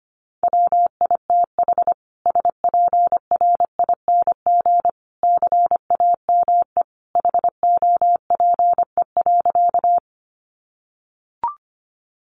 Morse code, heard as a keyed tone, about 25 wpm, 700 Hz. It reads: WITH SPRING CAME HOPE.
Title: Sentences from Top 1200 Words 25wpm